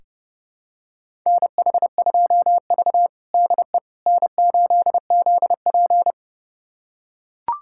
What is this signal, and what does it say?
Morse code, heard as a keyed tone, about 32 wpm, 700 Hz. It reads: NH2V DE N8ZP
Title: USA Callsign DE Callsign - 4 Characters - 30wpm